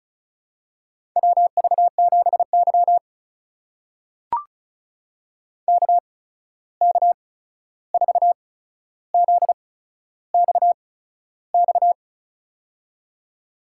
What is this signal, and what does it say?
Morse code, heard as a keyed tone, about 36 wpm, spaced as for 13 wpm, 700 Hz. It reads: WV7Y KK4ZXX
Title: USA Call Signs - 8x Character Spacing - 35wpm